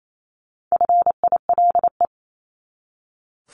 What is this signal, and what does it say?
Morse code, heard as a keyed tone, about 28 wpm, 700 Hz. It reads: FILE
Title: Top 1001-1250 Words 28wpm